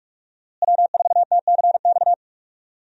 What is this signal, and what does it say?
Morse code, heard as a keyed tone, about 43 wpm, 700 Hz. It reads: W4TCX
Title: USA Call Signs - 5 Characters - Repeated 2x - 45wpm